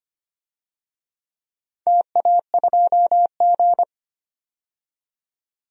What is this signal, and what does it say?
Morse code, heard as a keyed tone, about 25 wpm, 700 Hz. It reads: TA2G
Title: DXCC-100 Call Signs - 4 Characters - 25wpm